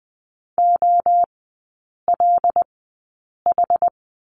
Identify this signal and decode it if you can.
Morse code, heard as a keyed tone, about 20 wpm, spaced as for 8 wpm, 690 Hz. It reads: OLH
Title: 081 Lesson 14e - Number 2 - Three Characters 20wpm